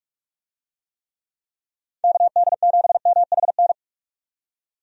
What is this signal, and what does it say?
Morse code, heard as a keyed tone, about 46 wpm, 700 Hz. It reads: KD7MHN
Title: USA Call Signs - 6 Characters - 45wpm